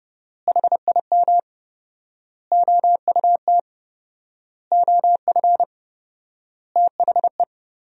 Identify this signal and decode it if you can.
Morse code, heard as a keyed tone, about 30 wpm, 700 Hz. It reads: HIM OUT OF THE
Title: 4x Word-Spacing - Sentences from Top 400 Words - 30wpm